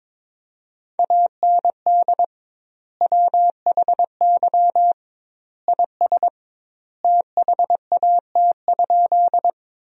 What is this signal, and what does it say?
Morse code, heard as a keyed tone, about 22 wpm, 700 Hz. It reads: AND WHY IS THAT?
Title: 2x Word-Spacing - Sentences from Top 200 Words - 22wpm